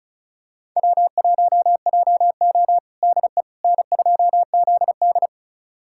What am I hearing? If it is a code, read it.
Morse code, heard as a keyed tone, about 36 wpm, 700 Hz. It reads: W1JO DE N2ZD
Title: USA Callsign DE Callsign - 4 Characters - 35wpm